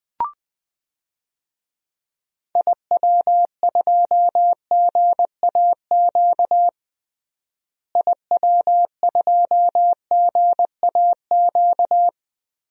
Morse code, heard as a keyed tone, about 21 wpm, 700 Hz. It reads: IW2GAQ IW2GAQ
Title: DXCC-100 Call Signs - 6 Characters - Repeated 3x - 20wpm